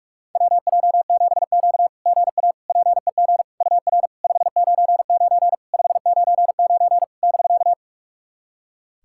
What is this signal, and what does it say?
Morse code, heard as a keyed tone, about 46 wpm, 690 Hz. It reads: WJ7Q GA PEG UR 599 599 <BK>
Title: SOTA QSO Practice 45wpm